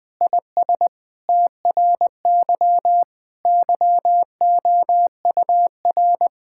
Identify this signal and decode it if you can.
Morse code, heard as a keyed tone, about 20 wpm, 700 Hz. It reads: IS TRY YOUR
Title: Sentences with 7-Word Phrases - 20wpm